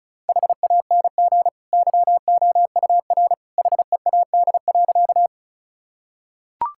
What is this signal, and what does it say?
Morse code, heard as a keyed tone, about 34 wpm, 700 Hz. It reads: HANG YOUR HEAD.
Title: Sentences from Top 1600 Words 35wpm